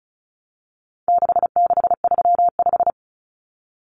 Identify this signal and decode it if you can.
Morse code, heard as a keyed tone, about 34 wpm, 700 Hz. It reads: 6635
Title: Numbers - 4-Digit 35wpm